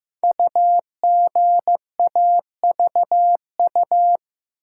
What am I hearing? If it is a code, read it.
Morse code, heard as a keyed tone, about 15 wpm, 700 Hz. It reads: UGAVU
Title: Code Groups - 5 Letter:Numbers - 15wpm